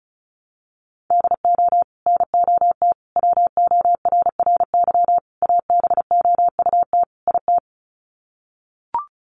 Morse code, heard as a keyed tone, about 34 wpm, 700 Hz. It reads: DO NOT WORRY ABOUT IT
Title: Sets of 5 Words 35wpm